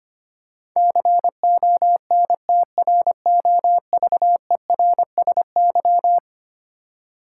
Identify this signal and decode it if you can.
Morse code, heard as a keyed tone, about 25 wpm, 700 Hz. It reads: CONTROVERSY
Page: morse-code-ninja-practice-25wpm